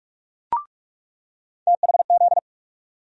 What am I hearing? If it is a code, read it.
Morse code, heard as a keyed tone, about 46 wpm, 700 Hz. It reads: THZ
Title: USA Call Sign Postfixes - 3 Letters 45wpm